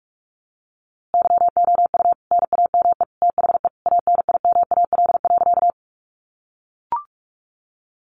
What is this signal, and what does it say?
Morse code, heard as a keyed tone, about 44 wpm, 700 Hz. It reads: YOU NAME THE ANIMAL.
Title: A Sound Of Thunder 45wpm